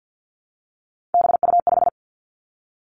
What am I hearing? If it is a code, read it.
Morse code, heard as a keyed tone, about 50 wpm, 700 Hz. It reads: BU5
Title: Three Challenging Letter-Number Combinations 50wpm